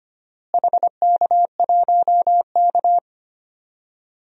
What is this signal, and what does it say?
Morse code, heard as a keyed tone, about 25 wpm, 700 Hz. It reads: HK1K
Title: DXCC-100 Call Signs - 4 Characters - Repeated 3x - 25wpm